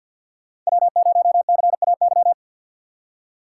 Morse code, heard as a keyed tone, about 63 wpm, 700 Hz. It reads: W0CAY